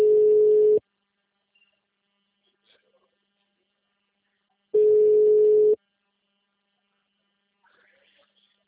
Ringback tone (Europe, 425 Hz).